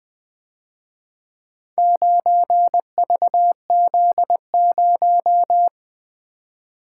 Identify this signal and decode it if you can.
Morse code, heard as a keyed tone, about 25 wpm, 700 Hz. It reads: 9VZ0
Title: Speed-Racing - Four Letter-Number Combinations - 20wpm